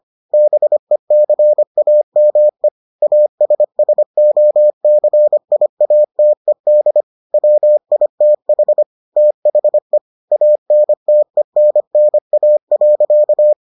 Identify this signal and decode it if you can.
Morse code, heard as a keyed tone, about 25 wpm, 600 Hz. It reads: BECAME ASSOCIATED WITH THE ANTENNA.